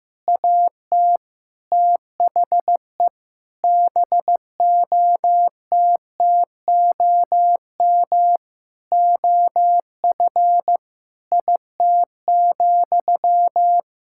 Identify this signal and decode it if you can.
Morse code, heard as a keyed tone, about 15 wpm, 700 Hz. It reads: AT THE BOTTOM OF IT,
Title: Sentences from Top 1000 Words - Review Part 6 - 15wpm